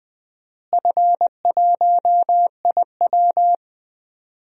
Morse code, heard as a keyed tone, about 20 wpm, 700 Hz. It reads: F1IW
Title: DXCC-100 Call Signs - 4 Characters - Repeated 3x - 20wpm